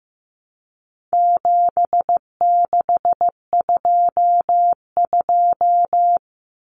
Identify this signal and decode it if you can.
Morse code, heard as a keyed tone, about 15 wpm, 700 Hz. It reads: 7622
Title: Common Cut Numbers - 4-Digit 15wpm